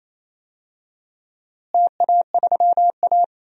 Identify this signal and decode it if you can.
Morse code, heard as a keyed tone, about 28 wpm, 700 Hz. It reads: TA3A
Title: DXCC-100 Call Signs - 4 Characters - Repeated 2x - 28wpm